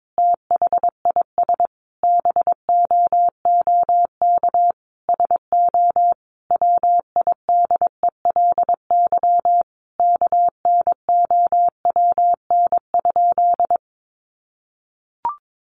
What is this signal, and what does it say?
Morse code, heard as a keyed tone, about 22 wpm, 700 Hz. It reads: THIS BOOK SO WIDELY KNOWN?
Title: Sentences from Top 1200 Words 22wpm